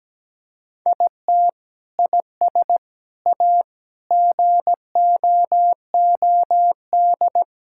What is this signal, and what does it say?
Morse code, heard as a keyed tone, about 17 wpm, 700 Hz. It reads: IT IS A GOOD
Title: Sentences from Top 1000 Words - Review Part 5 - 17wpm